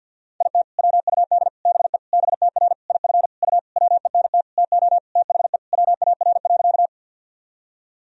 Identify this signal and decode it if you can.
Morse code, heard as a keyed tone, about 52 wpm, 700 Hz. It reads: IT WUD BE BTR IF U WENT TO THE WAR.
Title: Sentences from Top 400 Words and Abbreviations 50wpm